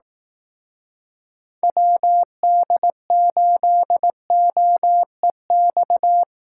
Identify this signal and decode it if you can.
Morse code, heard as a keyed tone, about 18 wpm, 700 Hz. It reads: WD8OEX
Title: Speed Racing - USA Call Signs 15wpm